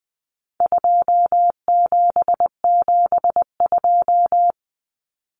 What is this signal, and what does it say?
Morse code, heard as a keyed tone, about 20 wpm, 700 Hz. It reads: 2772
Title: Common Cut Numbers - 4-Digit 20wpm